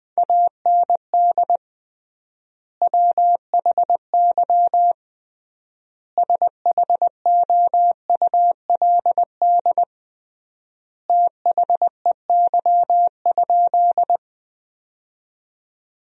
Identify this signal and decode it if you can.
Morse code, heard as a keyed tone, about 20 wpm, 700 Hz. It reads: AND WHY SHOULD THEY?
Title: 3x Word-Spacing - Sentences from Top 300 Words - 20wpm